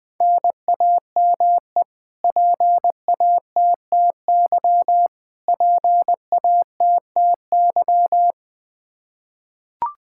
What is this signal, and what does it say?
Morse code, heard as a keyed tone, about 20 wpm, 700 Hz. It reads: NAME PATTY PATTY
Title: Practice QSO - Exchange 1 - Beginning - 20wpm